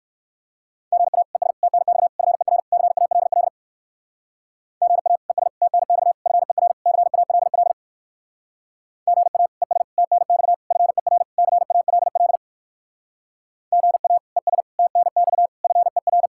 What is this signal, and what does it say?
Morse code, heard as a keyed tone, about 59 wpm, 700 Hz. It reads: GA ES TNX FER CALL GA ES TNX FER CALL GA ES TNX FER CALL GA ES TNX FER